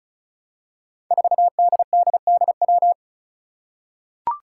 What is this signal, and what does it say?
Morse code, heard as a keyed tone, about 36 wpm, 700 Hz. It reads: 4DDDW
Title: Five Challenging Letter-Number Combinations 35wpm